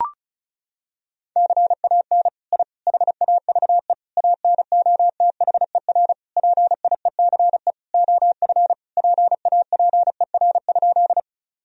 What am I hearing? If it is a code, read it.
Morse code, heard as a keyed tone, about 35 wpm, 700 Hz. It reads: CAN I HAVE ANOTHER PIECE OF PAPER?
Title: Sentences from Top 1000 Words - Review Part 2 - 35wpm